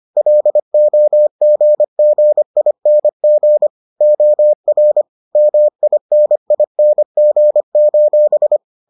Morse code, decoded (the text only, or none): LOGGING OR MINING: